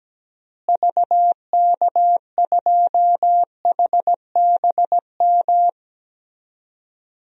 Morse code, heard as a keyed tone, about 17 wpm, 700 Hz. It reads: VK2HBM
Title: DXCC-100 Call Signs - 6 Characters - Repeated 3x - 17wpm